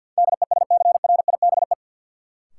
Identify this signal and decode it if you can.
Morse code, heard as a keyed tone, about 52 wpm, 700 Hz. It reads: DESCRIBE
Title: Sentences from Top 1000 Words - New Word Drill - 50wpm